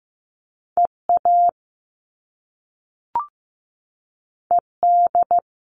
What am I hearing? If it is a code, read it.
Morse code, heard as a keyed tone, about 15 wpm, 700 Hz. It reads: EA ED
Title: All Cut Numbers - 2-Digit 15wpm